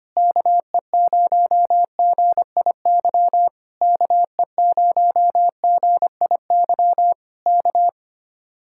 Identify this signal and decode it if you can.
Morse code, heard as a keyed tone, about 25 wpm, 700 Hz. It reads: KE0GIY KE0GIY K
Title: Speed-Racing Common QSO Phrases - 25wpm